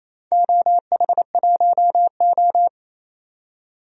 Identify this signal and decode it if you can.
Morse code, heard as a keyed tone, about 28 wpm, 700 Hz. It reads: OH1O
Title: DXCC-100 Call Signs - 4 Characters - 28wpm